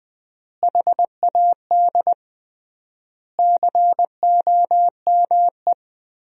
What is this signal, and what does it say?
Morse code, heard as a keyed tone, about 20 wpm, 700 Hz. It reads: HAD COME
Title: 3x Word-Spacing - Sentences from Top 500 Words - 20wpm